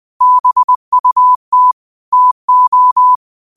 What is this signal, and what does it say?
Morse code, heard as a keyed tone, about 20 wpm, 1000 Hz. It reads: BUT TO